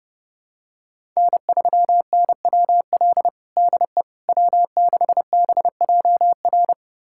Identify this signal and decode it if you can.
Morse code, heard as a keyed tone, about 30 wpm, 700 Hz. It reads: N3NWL DE W6BJR
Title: USA Callsign DE Callsign - 5 Characters - 30wpm